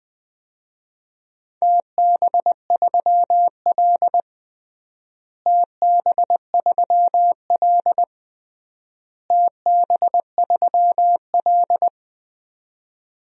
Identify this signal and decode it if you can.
Morse code, heard as a keyed tone, about 20 wpm, 700 Hz. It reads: TB3L TB3L TB3L
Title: DXCC-100 Call Signs - 4 Characters - Repeated 3x - 20wpm